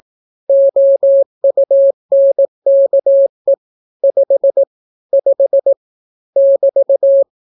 Morse code, heard as a keyed tone, about 18 wpm, 550 Hz. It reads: OUNKE 5 5 <BT>